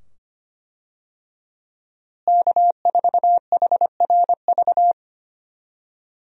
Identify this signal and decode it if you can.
Morse code, heard as a keyed tone, about 25 wpm, 700 Hz. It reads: K4HRV
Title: USA Call Signs - 5 Characters - Repeated 2x - 25wpm